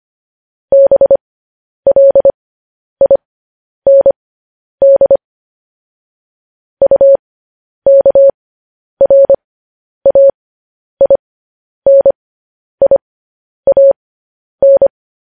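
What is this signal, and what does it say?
Morse code, heard as a keyed tone, about 25 wpm, spaced as for 10 wpm, 550 Hz. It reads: BLIND UKRAINIAN
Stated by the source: Morse code news headlines transmission.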